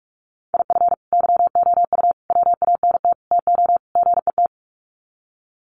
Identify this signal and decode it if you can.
Morse code, heard as a keyed tone, about 44 wpm, 700 Hz. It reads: IF YOU WANT TO GET
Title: Sets of 5 Words Encore 45wpm